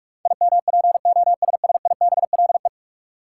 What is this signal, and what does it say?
Morse code, heard as a keyed tone, about 46 wpm, 700 Hz. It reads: IMPOSSIBLE
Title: Sentences from Top 1100 Words - New Word Drill - 45wpm